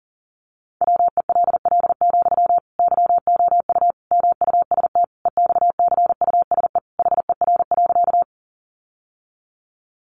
Morse code, heard as a keyed tone, about 40 wpm, 700 Hz. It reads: WELL, YOU MUST EXCUSE HER.
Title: Sentences from Top 700 Words 40wpm